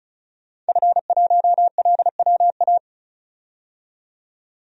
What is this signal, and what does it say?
Morse code, heard as a keyed tone, about 34 wpm, 700 Hz. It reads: F1LWA
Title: DXCC-100 Call Signs - 5 Characters - 35wpm